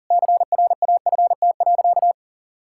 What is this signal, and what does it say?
Morse code, heard as a keyed tone, about 38 wpm, 700 Hz. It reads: CRAFT.
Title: Sentences from Top 1000 Words - Review Part 6 - 40wpm